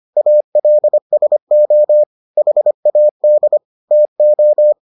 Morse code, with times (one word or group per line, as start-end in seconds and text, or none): 0.2-2.0 ALSO
2.4-3.6 HAD
3.9-4.7 TO